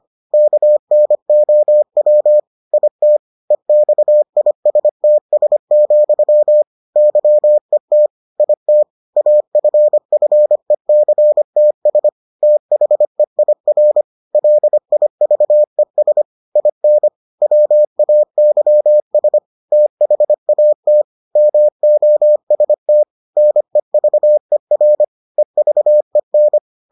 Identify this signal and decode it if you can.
Morse code, heard as a keyed tone, about 25 wpm, 600 Hz. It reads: KNOW IT EXISTS, YET IT AFFECTS THEIR LIVES IN WAYS THAT MOST NEVER EVEN